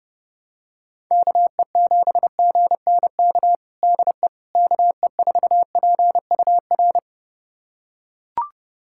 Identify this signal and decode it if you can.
Morse code, heard as a keyed tone, about 32 wpm, 700 Hz. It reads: KE7GNK DE KE4PUR
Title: USA Callsign DE Callsign - 6 Characters - 30wpm